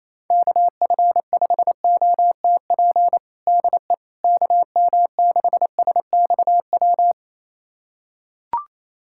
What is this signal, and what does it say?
Morse code, heard as a keyed tone, about 28 wpm, 700 Hz. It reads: KF5OTP DE KM6SXW
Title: USA Callsign DE Callsign 28wpm